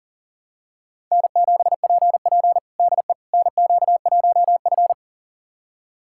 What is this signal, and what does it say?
Morse code, heard as a keyed tone, about 37 wpm, 700 Hz. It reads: N7PP DE NQ1F
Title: USA Callsign DE Callsign - 4 Characters - 40wpm